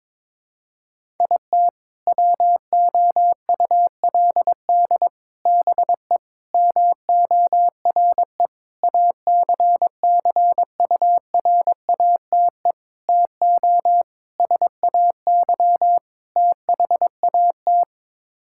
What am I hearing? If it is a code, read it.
Morse code, heard as a keyed tone, about 22 wpm, 700 Hz. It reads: IT WOULD BE MORE ACCURATE TO SAY THAT
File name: practice / 22wpm / Sets of 8 Words - 22wpm